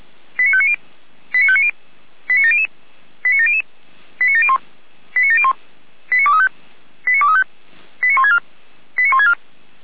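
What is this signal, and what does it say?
Five-tone selective call (ZVEI), heard as quick runs of five tones, tones of about 70 ms. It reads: 89590 89590 89700 89700 89701 89701 89235 89235 89164 89164
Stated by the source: ZVEI five-tone selective call received by SDR.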